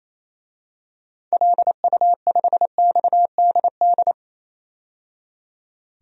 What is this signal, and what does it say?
Morse code, heard as a keyed tone, about 28 wpm, 700 Hz. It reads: LU5XDD
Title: DXCC-100 Call Signs - 6 Characters - 28wpm